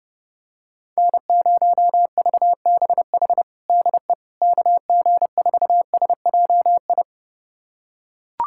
Morse code, heard as a keyed tone, about 30 wpm, 700 Hz. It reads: N0VBH DE KG4SJI
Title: USA Callsign DE Callsign 30wpm